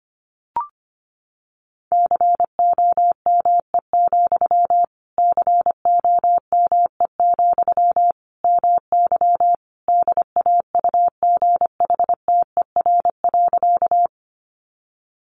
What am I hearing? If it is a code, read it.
Morse code, heard as a keyed tone, about 25 wpm, 700 Hz. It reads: COME, COME, MY DAUGHTER.
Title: Sentences from Top 1500 Words 25wpm